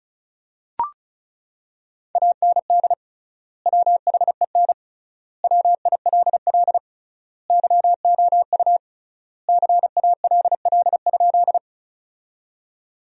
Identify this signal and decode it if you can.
Morse code, heard as a keyed tone, about 36 wpm, 700 Hz. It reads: AND WHEN WILL YOU CALL?
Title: 3x Word-Spacing - Sentences from Top 100 Words - 35wpm